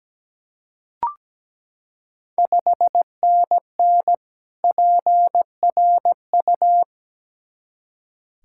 Morse code, heard as a keyed tone, about 17 wpm, 700 Hz. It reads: 5NN PRU
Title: Ontario QSO Party Exchange 17wpm